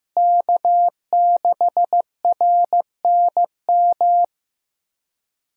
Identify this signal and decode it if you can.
Morse code, heard as a keyed tone, about 15 wpm, 700 Hz. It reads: K6RNM
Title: USA Call Signs - 5 Characters - Repeated 3x - 15wpm